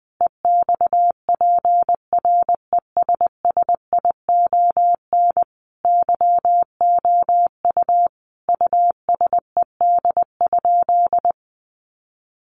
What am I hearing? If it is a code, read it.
Morse code, heard as a keyed tone, about 20 wpm, 700 Hz. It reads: EXPRESSION YOU USED?
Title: Sentences from Top 1200 Words 20wpm